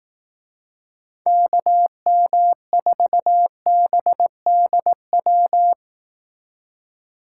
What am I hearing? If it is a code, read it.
Morse code, heard as a keyed tone, about 18 wpm, 700 Hz. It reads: KM4BDW
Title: Speed Racing - USA Call Signs 15wpm